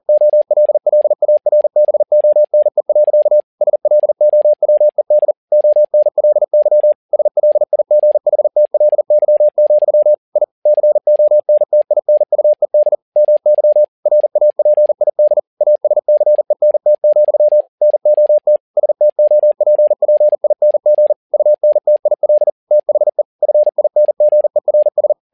Morse code, heard as a keyed tone, about 40 wpm, 600 Hz. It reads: OLLARBONE. SLOWED ONLY SLIGHTLY, I CONTINUED MY RAPID ASCENT, NOT STOPPING UNTIL THE FINGERS
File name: learning center / tower trouble / tower touble 40wpm